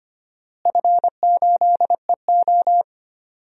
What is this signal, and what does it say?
Morse code, heard as a keyed tone, about 25 wpm, 700 Hz. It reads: F8EO
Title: DXCC-100 Call Signs - 4 Characters - Repeated 3x - 25wpm